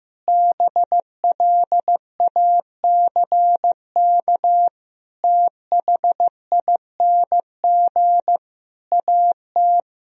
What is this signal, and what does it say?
Morse code, heard as a keyed tone, about 15 wpm, 700 Hz. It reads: BLACK THING AT
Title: Sentences from Top 1000 Words - Review Part 2 - 15wpm